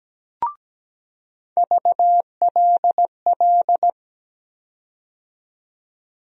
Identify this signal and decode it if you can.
Morse code, heard as a keyed tone, about 17 wpm, 700 Hz. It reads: VLL
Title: USA Call Sign Postfixes - 3 Letters 17wpm